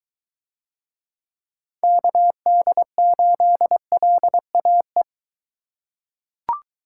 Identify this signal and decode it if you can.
Morse code, heard as a keyed tone, about 23 wpm, 700 Hz. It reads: KD8LAE